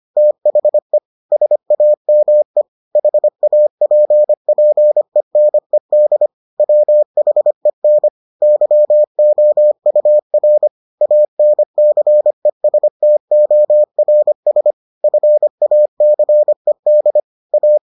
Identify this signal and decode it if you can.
Morse code, heard as a keyed tone, about 25 wpm, 600 Hz. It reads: THE SAME HAPPENED WHEN YOUR ANCESTORS FACED A